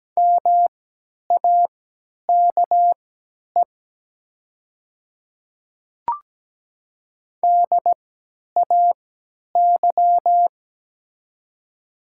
Morse code, heard as a keyed tone, about 17 wpm, spaced as for 10 wpm, 700 Hz. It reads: MAKE DAY
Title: 3x Character Spacing - Top 100 Words - 17wpm